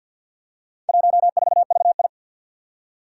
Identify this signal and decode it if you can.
Morse code, heard as a keyed tone, about 50 wpm, 700 Hz. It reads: 14VI
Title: Four Challenging Letter-Number Combinations 50wpm